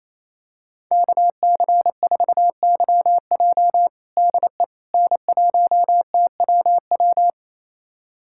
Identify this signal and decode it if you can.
Morse code, heard as a keyed tone, about 28 wpm, 700 Hz. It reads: KC4YJ DE N1TWW
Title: USA Callsign DE Callsign - 5 Characters - 28wpm